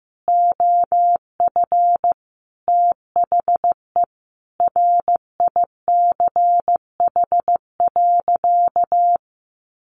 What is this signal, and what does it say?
Morse code, heard as a keyed tone, about 15 wpm, 700 Hz. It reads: OF THE RICH.
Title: Sentences from Top 1100 Words 15wpm